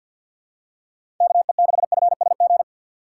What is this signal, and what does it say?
Morse code, heard as a keyed tone, about 52 wpm, 700 Hz. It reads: KE6FSG
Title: USA Call Signs - 6 Characters - 3x Character Spacing - 50wpm